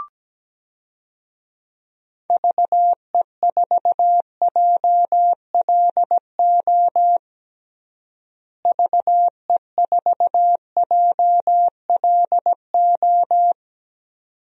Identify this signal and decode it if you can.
Morse code, heard as a keyed tone, about 17 wpm, 700 Hz. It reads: VE4JLO VE4JLO
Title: DXCC-100 Call Signs - 6 Characters - Repeated 2x - 17wpm